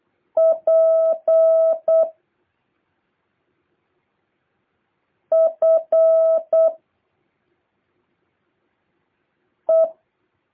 Morse code, heard as a keyed tone, about 8 wpm, 650 Hz. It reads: P F E
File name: jh8cbh / 2023cw